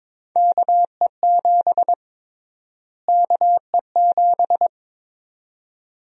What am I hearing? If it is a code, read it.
Morse code, heard as a keyed tone, about 22 wpm, 700 Hz. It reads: KE7 KE7
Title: USA Call Sign Prefixes - Repeated 2x - 22wpm